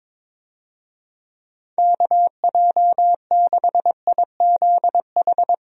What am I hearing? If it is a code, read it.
Morse code, heard as a keyed tone, about 22 wpm, 700 Hz. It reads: KJ6IZH
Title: USA Call Signs - 6 Characters - Repeated 2x - 22wpm